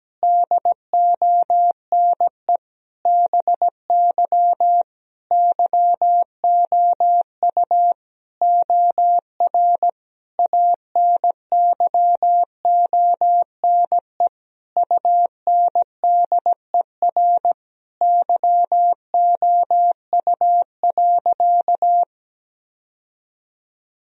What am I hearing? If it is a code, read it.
Morse code, heard as a keyed tone, about 17 wpm, 700 Hz. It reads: DONE BY YOU OR ANYONE UNDER YOU.
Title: Sentences from Top 1000 Words - Review Part 1 - 17wpm